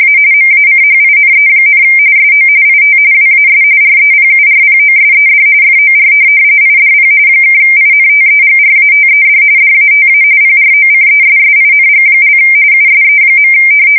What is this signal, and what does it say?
RTTY, about 45.45 baud. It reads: A MACHINE, SO IT WAS REPURPOSED FOR IDP (INTEGRATED DATA PROCESSING) USE.
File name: fsk170 / teletype-corp